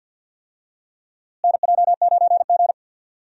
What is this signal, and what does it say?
Morse code, heard as a keyed tone, about 52 wpm, 700 Hz. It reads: NJ9G
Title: USA Call Signs - 4 Characters - Repeated 2x - 50wpm